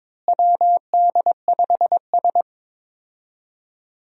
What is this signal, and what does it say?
Morse code, heard as a keyed tone, about 22 wpm, 700 Hz. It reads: WD5S
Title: USA Call Signs - 4 Characters - Repeated 2x - 22wpm